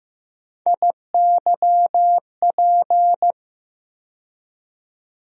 Morse code, heard as a keyed tone, about 15 wpm, 700 Hz. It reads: IYP
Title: USA Call Sign Postfixes - 3 Letters 15wpm